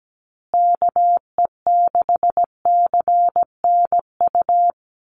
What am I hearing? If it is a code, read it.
Morse code, heard as a keyed tone, about 17 wpm, 700 Hz. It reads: KE6CNU
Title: USA Call Signs Repeated 2x 17wpm